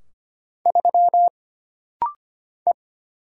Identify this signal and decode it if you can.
Morse code, heard as a keyed tone, about 25 wpm, 700 Hz. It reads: 3 E
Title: Single Letter-Number - Rapid-Fire 25wpm